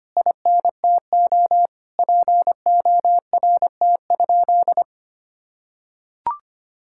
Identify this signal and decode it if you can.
Morse code, heard as a keyed tone, about 25 wpm, 700 Hz. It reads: INTO PORT?
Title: Sentences from Top 1000 Words - Review Part 5 - 25wpm